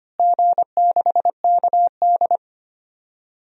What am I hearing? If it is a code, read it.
Morse code, heard as a keyed tone, about 25 wpm, 700 Hz. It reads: G6KD
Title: DXCC-100 Call Signs - 4 Characters - Repeated 3x - 25wpm